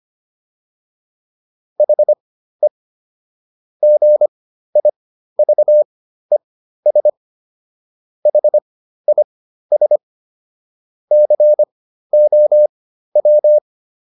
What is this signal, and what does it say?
Morse code, heard as a keyed tone, about 25 wpm, spaced as for 13 wpm, 600 Hz. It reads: HE GIVES HIS COW